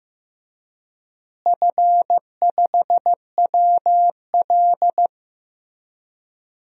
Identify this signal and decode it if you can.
Morse code, heard as a keyed tone, about 15 wpm, 700 Hz. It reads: F5WL
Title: DXCC-100 Call Signs - 4 Characters - Repeated 3x - 15wpm